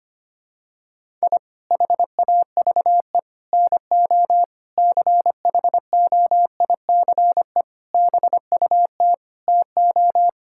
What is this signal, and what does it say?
Morse code, heard as a keyed tone, about 25 wpm, 700 Hz. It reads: I HAVE NO CHOICE BUT TO
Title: Sentences from Top 1000 Words - Review Part 6 - 25wpm